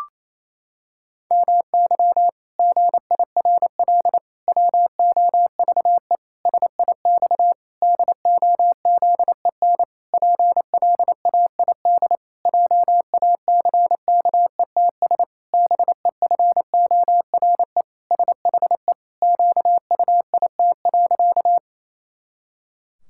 Morse code, heard as a keyed tone, about 28 wpm, 700 Hz. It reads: MY GIRL WOVE SIX DOZEN PLAID JACKETS BEFORE SHE QUIT.
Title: Pangram Challenge v2 - 28wpm